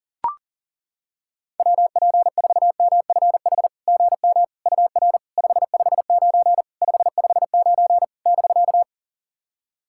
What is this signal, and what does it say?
Morse code, heard as a keyed tone, about 40 wpm, 700 Hz. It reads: WP4MFH GM UR 559 559 <BK>
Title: SOTA QSO Practice 40wpm